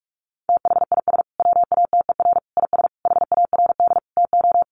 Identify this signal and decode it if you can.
Morse code, heard as a keyed tone, about 44 wpm, 700 Hz. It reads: THIS WATER IS HARD TO
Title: Sentences from Top 1300 Words 45wpm